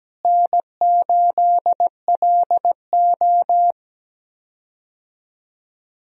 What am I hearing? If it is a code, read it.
Morse code, heard as a keyed tone, about 17 wpm, 700 Hz. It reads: N8LO
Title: USA Call Signs - 4 Characters - Repeated 2x - 17wpm